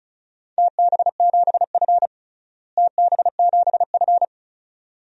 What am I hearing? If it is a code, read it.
Morse code, heard as a keyed tone, about 34 wpm, 700 Hz. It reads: TB7F TB7F
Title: DXCC-100 Call Signs - 4 Characters - Repeated 2x - 35wpm